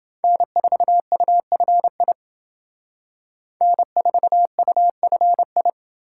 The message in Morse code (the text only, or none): N4UFI N4UFI